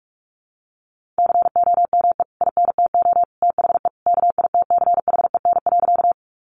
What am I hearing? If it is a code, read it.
Morse code, heard as a keyed tone, about 44 wpm, 700 Hz. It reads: COME INTO THE KITCHEN.
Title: Sentences from Top 1400 Words 45wpm